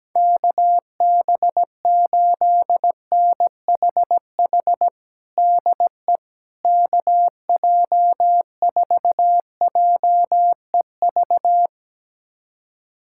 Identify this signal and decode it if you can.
Morse code, heard as a keyed tone, about 17 wpm, 700 Hz. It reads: KB8NHH DE KJ4JEV
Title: USA Callsign DE Callsign 17wpm